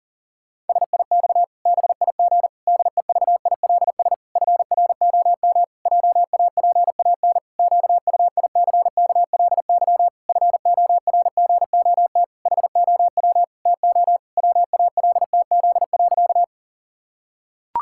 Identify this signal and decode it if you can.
Morse code, heard as a keyed tone, about 40 wpm, 700 Hz. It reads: SIX BIG DEVILS FROM JAPAN QUICKLY FORGOT HOW TO WALTZ.
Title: Pangram Challenge v2 - 40wpm